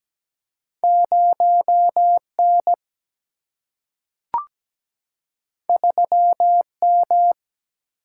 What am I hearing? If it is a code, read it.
Morse code, heard as a keyed tone, about 17 wpm, 700 Hz. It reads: 0N 3M
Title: Single-Letter Single-Number 17wpm